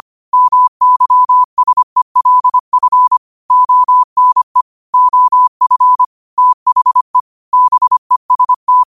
Morse code, heard as a keyed tone, about 25 wpm, 1000 Hz. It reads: MYSELF ONE OF THE BEST
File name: qotdfiles / 25wpm